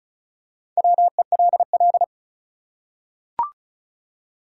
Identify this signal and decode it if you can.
Morse code, heard as a keyed tone, about 36 wpm, 700 Hz. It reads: WELL
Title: Top 180 QSO Elements 35wpm